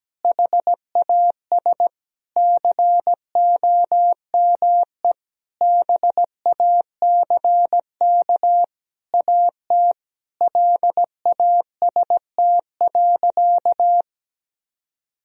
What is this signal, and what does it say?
Morse code, heard as a keyed tone, about 17 wpm, 700 Hz. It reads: HAS COME BACK AT LAST.
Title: Sentences from Top 1000 Words - Review Part 2 - 17wpm